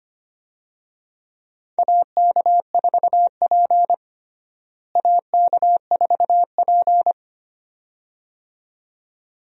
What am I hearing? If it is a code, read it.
Morse code, heard as a keyed tone, about 25 wpm, 700 Hz. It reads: AK4P AK4P
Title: USA Call Signs - 4 Characters - Repeated 2x - 25wpm